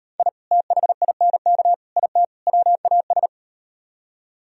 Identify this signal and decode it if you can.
Morse code, heard as a keyed tone, about 38 wpm, 700 Hz. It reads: I THINK IT WAS